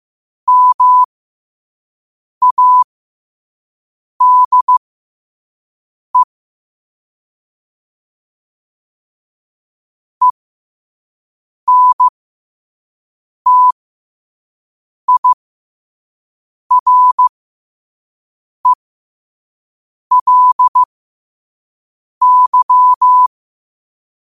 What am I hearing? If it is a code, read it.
Morse code, heard as a keyed tone, about 15 wpm, spaced as for 5 wpm, 1000 Hz. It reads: MADE ENTIRELY
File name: qotdfiles / 5wpm